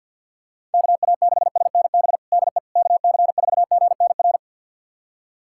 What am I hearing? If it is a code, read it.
Morse code, heard as a keyed tone, about 51 wpm, 700 Hz. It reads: KA6SNB DE KC4GNR
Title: USA Callsign DE Callsign - 6 Characters - 50wpm